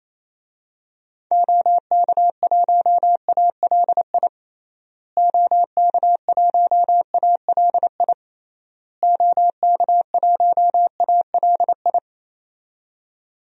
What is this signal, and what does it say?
Morse code, heard as a keyed tone, about 27 wpm, 700 Hz. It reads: OK1ALI OK1ALI OK1ALI
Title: DXCC-100 Call Signs - 6 Characters - Repeated 3x - 28wpm